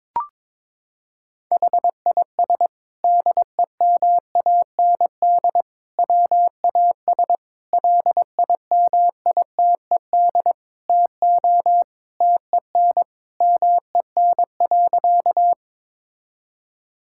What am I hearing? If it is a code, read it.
Morse code, heard as a keyed tone, about 22 wpm, 700 Hz. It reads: HIS DEMAND WAS LIMITED TO TEN MEN.
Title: Sentences from Top 1000 Words - Review Part 6 - 22wpm